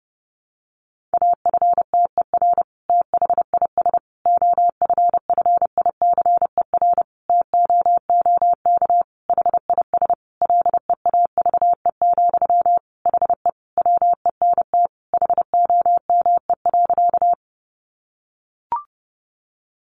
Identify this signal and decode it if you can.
Morse code, heard as a keyed tone, about 30 wpm, 700 Hz. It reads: AFTER THIS OFFICER TOOK HIS LEAVE, HE WENT HOME.
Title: Sentences from Top 1200 Words 30wpm